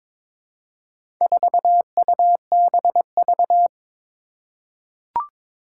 Morse code, heard as a keyed tone, about 22 wpm, 700 Hz. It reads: 4UBV
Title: Four Challenging Letter-Number Combinations 22wpm